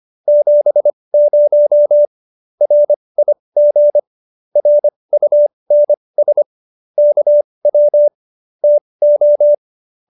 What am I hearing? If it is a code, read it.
Morse code, heard as a keyed tone, about 25 wpm, spaced as for 20 wpm, 590 Hz. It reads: 70 RIG RUNS KW TO
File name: QSO307 20